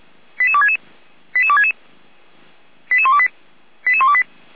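Five-tone selective call (ZVEI), heard as quick runs of five tones, tones of about 70 ms. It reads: 88266 88266 88128 88128
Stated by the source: ZVEI five-tone selective call, received on SDR.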